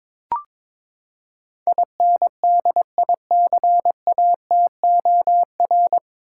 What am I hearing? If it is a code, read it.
Morse code, heard as a keyed tone, about 21 wpm, 700 Hz. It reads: INDICATOR
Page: Morse Code Ninja practice 22wpm